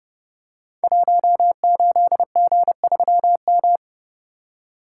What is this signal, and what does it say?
Morse code, heard as a keyed tone, about 30 wpm, 700 Hz. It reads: 18G3M
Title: Code Groups - 5 Letter:Numbers - 30wpm